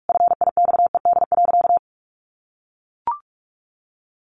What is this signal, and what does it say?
Morse code, heard as a keyed tone, about 44 wpm, 700 Hz. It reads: FIXED.
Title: Sentences from Top 1100 Words 45wpm